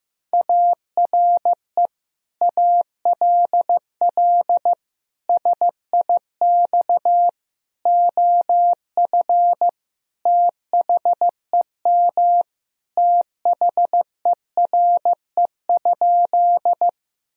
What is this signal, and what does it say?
Morse code, heard as a keyed tone, about 15 wpm, 700 Hz. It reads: ARE ALL SIX OF THEM THERE?
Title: Sentences from Top 1000 Words - Review Part 2 - 15wpm